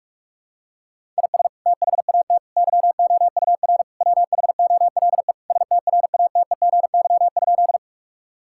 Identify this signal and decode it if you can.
Morse code, heard as a keyed tone, about 46 wpm, 700 Hz. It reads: IS THAT YOUR WHOLE STRATEGY?
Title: Sentences from Top 1000 Words - Review Part 5 - 45wpm